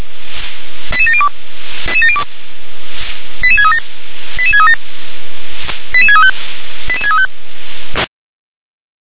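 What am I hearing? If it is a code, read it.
Five-tone selective call (ZVEI), heard as quick runs of five tones, tones of about 70 ms. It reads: 88702 88637 88637 88635 88635
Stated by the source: ZVEI five-tone selective call received by SDR.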